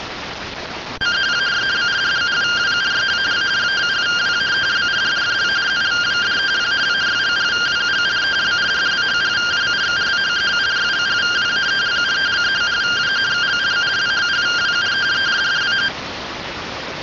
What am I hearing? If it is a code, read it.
Two-tone teleprinter signal, about 170 Hz shift.